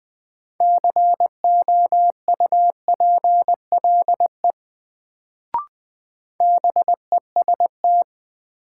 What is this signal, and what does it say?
Morse code, heard as a keyed tone, about 20 wpm, 700 Hz. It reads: COUPLE BEST
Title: Sets of 2 Words - New Word Drill - 20wpm